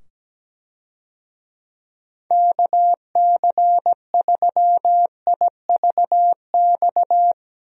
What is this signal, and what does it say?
Morse code, heard as a keyed tone, about 17 wpm, 700 Hz. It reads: KC3IVX
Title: USA Call Signs - 6 Characters - 17wpm